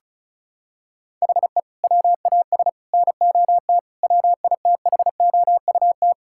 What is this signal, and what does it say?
Morse code, heard as a keyed tone, about 34 wpm, 700 Hz. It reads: HE WAS NOT WITHOUT
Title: Sentences from Top 1000 Words - Review Part 6 - 35wpm